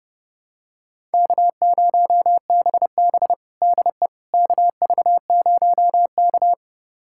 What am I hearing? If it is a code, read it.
Morse code, heard as a keyed tone, about 32 wpm, 700 Hz. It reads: K0BB DE KV0K
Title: USA Callsign DE Callsign - 4 Characters - 30wpm